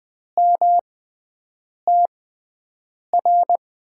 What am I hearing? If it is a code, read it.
Morse code, heard as a keyed tone, about 21 wpm, spaced as for 7 wpm, 700 Hz. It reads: MTR
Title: USA Call Signs - 6 Characters - 6x Character Spacing - 20wpm